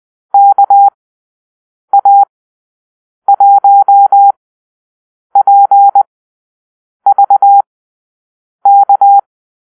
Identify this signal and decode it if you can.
Morse code, heard as a keyed tone, about 20 wpm, spaced as for 7 wpm, 800 Hz. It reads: KA1PVK
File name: arrl / Call-Sign Drill-8